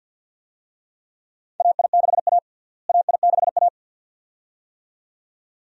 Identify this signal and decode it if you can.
Morse code, heard as a keyed tone, about 52 wpm, 700 Hz. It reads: AI6A AI6A
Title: USA Call Signs - 4 Characters - Repeated 2x - 50wpm